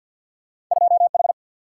Morse code, heard as a keyed tone, about 50 wpm, 700 Hz. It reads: 2H
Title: Single-Letter Single-Number 50wpm